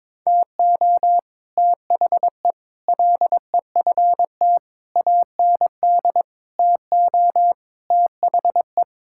Morse code, heard as a keyed tone, about 22 wpm, 700 Hz. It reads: TO THE LEFT AND TO THE
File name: Sentences with 9-Word Phrases - 22wpm